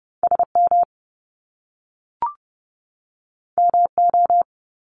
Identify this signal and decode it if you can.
Morse code, heard as a keyed tone, about 30 wpm, 700 Hz. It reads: SM MO
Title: Top Two Letter Combinations in Words 30wpm